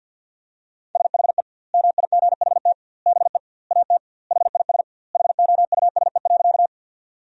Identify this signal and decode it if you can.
Morse code, heard as a keyed tone, about 50 wpm, 700 Hz. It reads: SHE MIGHT BE AT HIS HOUSE.
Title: 2x Word-Spacing - Sentences from Top 300 Words - 50wpm